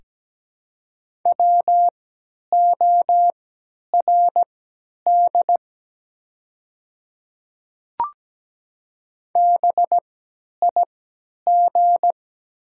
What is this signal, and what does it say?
Morse code, heard as a keyed tone, about 17 wpm, spaced as for 10 wpm, 700 Hz. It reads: WORD BIG
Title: 3x Character Spacing - Top 500 Words - 17wpm